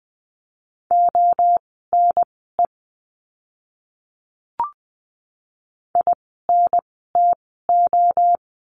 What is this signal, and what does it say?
Morse code, heard as a keyed tone, about 20 wpm, spaced as for 15 wpm, 700 Hz. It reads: ONE INTO
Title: 244 Lesson 39f - Letter z - Words 20wpm